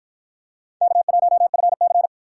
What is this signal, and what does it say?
Morse code, heard as a keyed tone, about 52 wpm, 700 Hz. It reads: K1FC